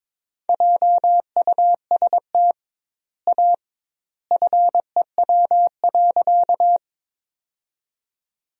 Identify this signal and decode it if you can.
Morse code, heard as a keyed tone, about 21 wpm, 700 Hz. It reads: JUST A FEW.
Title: 2x Word-Spacing - Sentences from Top 300 Words - 22wpm